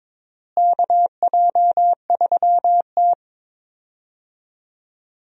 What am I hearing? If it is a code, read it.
Morse code, heard as a keyed tone, about 22 wpm, 700 Hz. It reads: KJ3T
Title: USA Call Signs - 4 Characters - Repeated 3x - 22wpm